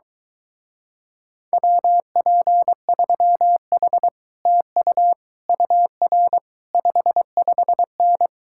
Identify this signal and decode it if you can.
Morse code, heard as a keyed tone, about 22 wpm, 700 Hz. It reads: WP3H TU UR 55N